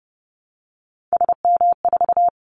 Morse code, heard as a keyed tone, about 30 wpm, 700 Hz. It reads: SM4
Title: Callsign Prefixes for Top 25 DXCC - 30wpm